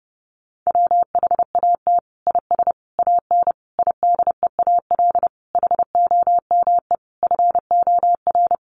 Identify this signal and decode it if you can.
Morse code, heard as a keyed tone, about 30 wpm, 700 Hz. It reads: WHAT IS AN IDEAL HOME FOR
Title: Sentences from Top 1400 Words 30wpm